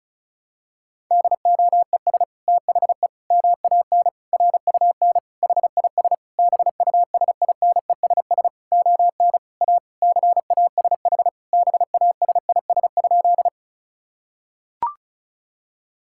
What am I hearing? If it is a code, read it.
Morse code, heard as a keyed tone, about 36 wpm, 700 Hz. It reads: DOES THE MAN RUN HIS BUSINESS ON A CASH BASIS?
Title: Sentences from Top 1000 Words - Review Part 5 - 35wpm